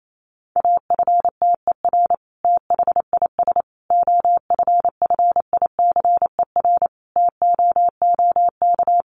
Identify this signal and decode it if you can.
Morse code, heard as a keyed tone, about 28 wpm, 700 Hz. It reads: AFTER THIS OFFICER TOOK
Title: Sentences from Top 1200 Words 28wpm